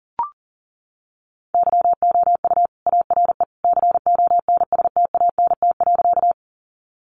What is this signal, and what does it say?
Morse code, heard as a keyed tone, about 40 wpm, 700 Hz. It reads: YOU ARE CONSTANT.
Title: Sentences from Top 1500 Words 40wpm